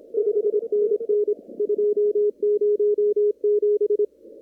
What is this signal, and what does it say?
Morse code, heard as a keyed tone, about 26 wpm, 1180 Hz. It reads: 5NN 207
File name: CW / kt5j wpx cw-op-k5pi